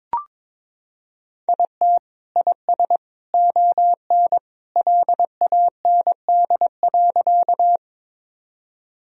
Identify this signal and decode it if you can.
Morse code, heard as a keyed tone, about 22 wpm, 700 Hz. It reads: IT IS ON LAND.
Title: Sentences from Top 1000 Words - Review Part 1 - 22wpm